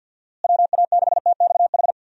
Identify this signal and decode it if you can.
Morse code, heard as a keyed tone, about 47 wpm, 700 Hz. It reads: WA6TXH
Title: USA Call Signs - 6 Characters - Repeated 2x - 50wpm